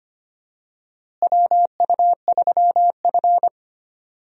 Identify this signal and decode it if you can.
Morse code, heard as a keyed tone, about 25 wpm, 700 Hz. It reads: WU3F
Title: USA Call Signs - 4 Characters - Repeated 3x - 25wpm